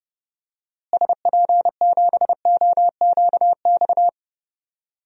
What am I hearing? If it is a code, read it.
Morse code, heard as a keyed tone, about 32 wpm, 700 Hz. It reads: SP7OQX
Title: DXCC-100 Call Signs - 6 Characters - Repeated 2x - 30wpm